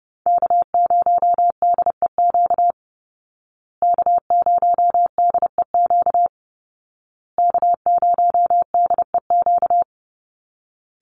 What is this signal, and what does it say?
Morse code, heard as a keyed tone, about 30 wpm, 700 Hz. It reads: K0DEQ K0DEQ K0DEQ
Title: Call Signs Repeated 2x - 30wpm